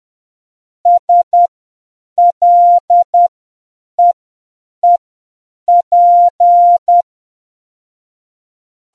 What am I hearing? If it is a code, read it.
Morse code, heard as a keyed tone, about 10 wpm, spaced as for 7 wpm, 700 Hz. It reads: SLEEP